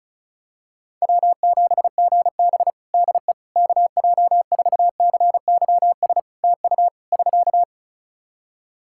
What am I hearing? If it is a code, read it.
Morse code, heard as a keyed tone, about 36 wpm, 700 Hz. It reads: W7GB DE KJ4CYS TU <SK>
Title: Practice QSO - Exchange 4 - Ending - 35wpm